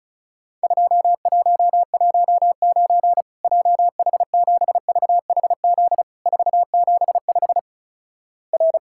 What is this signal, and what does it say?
Morse code, heard as a keyed tone, about 34 wpm, 700 Hz. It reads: 2119 JH7VHZ 475 R
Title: Worked All Europe DX Contest - QTC Exchange - 35wpm